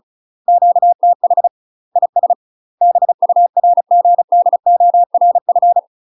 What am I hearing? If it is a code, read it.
Morse code, heard as a keyed tone, about 36 wpm, 700 Hz. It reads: QTH IS BURGDORF
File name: UK MASTER QSOs TRAINING / 35WPM / UK MASTER QSOs 0043